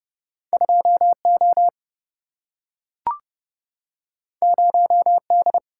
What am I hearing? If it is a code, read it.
Morse code, heard as a keyed tone, about 31 wpm, 700 Hz. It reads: 2 O 0 D
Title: Single-Letter Single-Number 30wpm